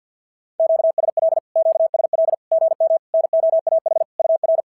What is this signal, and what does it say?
Morse code, heard as a keyed tone, about 51 wpm, 690 Hz. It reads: QSL QSL GM NOAH UR
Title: SOTA QSO Practice 50wpm